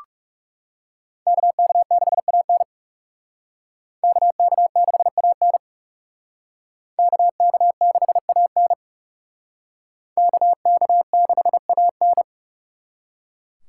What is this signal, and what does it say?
Morse code, heard as a keyed tone, about 37 wpm, 700 Hz. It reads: KK6AN KK6AN KK6AN KK6AN
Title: Speed Racing - USA 2x2 Call Signs 30wpm